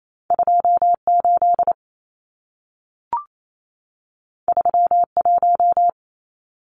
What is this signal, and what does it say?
Morse code, heard as a keyed tone, about 28 wpm, 700 Hz. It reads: 28 31
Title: Numbers - 2-Digit 28wpm